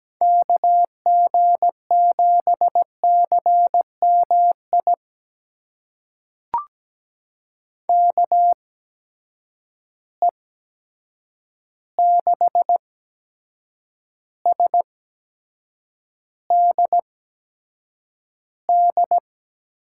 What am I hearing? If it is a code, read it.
Morse code, heard as a keyed tone, about 17 wpm, spaced as for 7 wpm, 700 Hz. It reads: KG7CMI KE6SDD
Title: USA Call Signs - 6 Characters - 8x Character Spacing - 17wpm